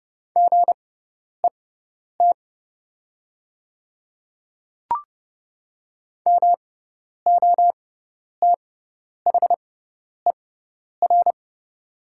Morse code, heard as a keyed tone, about 30 wpm, spaced as for 10 wpm, 700 Hz. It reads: GET MOTHER
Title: 6x Character Spacing - Top 200 Words - 30wpm